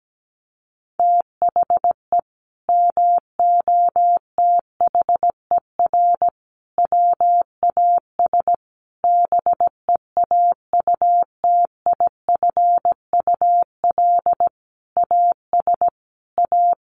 Morse code, heard as a keyed tone, about 17 wpm, 700 Hz. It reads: THE MOTHER WAS BEAUTIFUL AS A
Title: Sentences from Top 1500 Words 17wpm